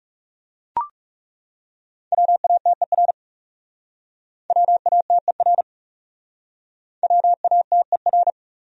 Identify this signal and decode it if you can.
Morse code, heard as a keyed tone, about 41 wpm, 700 Hz. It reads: WATER WATER WATER